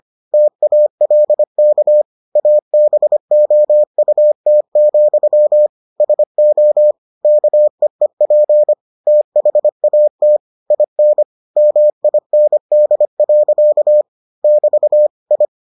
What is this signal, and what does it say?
Morse code, heard as a keyed tone, about 25 wpm, 600 Hz. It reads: TALK ABOUT, SO KEEP THAT IN MIND. <BT> I